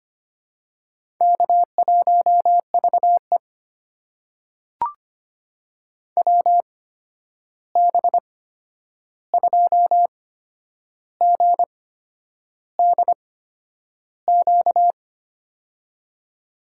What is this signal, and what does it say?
Morse code, heard as a keyed tone, about 25 wpm, spaced as for 9 wpm, 700 Hz. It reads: K1VE WB2GDQ
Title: USA Call Signs - 8x Character Spacing - 25wpm